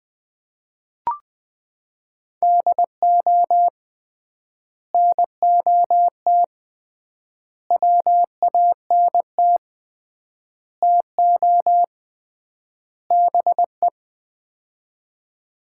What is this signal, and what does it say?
Morse code, heard as a keyed tone, about 20 wpm, 700 Hz. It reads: DO NOT WANT TO BE
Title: 3x Word-Spacing - Sets of 5 Words - 20wpm